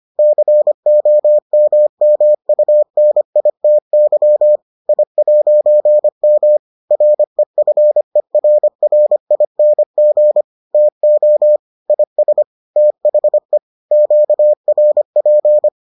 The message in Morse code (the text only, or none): COMMUNITY I'M REFERRING TO IS THE QRP